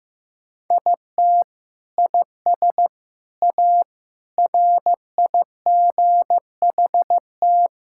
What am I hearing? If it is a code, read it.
Morse code, heard as a keyed tone, about 16 wpm, 700 Hz. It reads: IT IS A RIGHT
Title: Sentences from Top 1000 Words - Review Part 5 - 15wpm